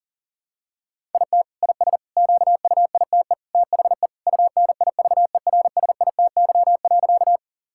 Morse code, heard as a keyed tone, about 38 wpm, 700 Hz. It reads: IT IS QUITE THE UNIVERSITY.
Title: Sentences from Top 1000 Words - Review Part 5 - 40wpm